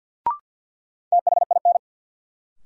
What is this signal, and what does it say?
Morse code, heard as a keyed tone, about 53 wpm, 700 Hz. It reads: THIN
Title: Sentences from Top 1200 Words - New Word Drill Alt - 50wpm